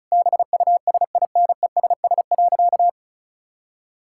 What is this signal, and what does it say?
Morse code, heard as a keyed tone, about 36 wpm, 700 Hz. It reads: BUSINESS.
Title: Sentences from Top 1000 Words - Review Part 6 - 35wpm